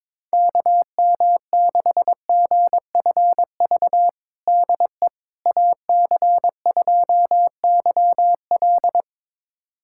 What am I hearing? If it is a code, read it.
Morse code, heard as a keyed tone, about 22 wpm, 700 Hz. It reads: KM6GFV DE AC2YL
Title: USA Callsign DE Callsign 22wpm